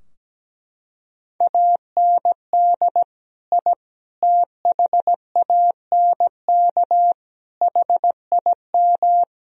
Morse code, heard as a keyed tone, about 17 wpm, 700 Hz. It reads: AND I THANK HIM
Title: Sentences from Top 1000 Words - Review Part 3 - 17wpm